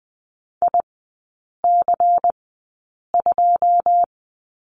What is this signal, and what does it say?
Morse code, heard as a keyed tone, about 20 wpm, spaced as for 8 wpm, 690 Hz. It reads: IC2
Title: 212 Lesson 34e - Number 8 - Three Characters 20wpm